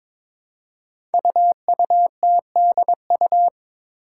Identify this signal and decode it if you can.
Morse code, heard as a keyed tone, about 22 wpm, 700 Hz. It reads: UUTDU
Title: Code Groups - 5 Letters - 22wpm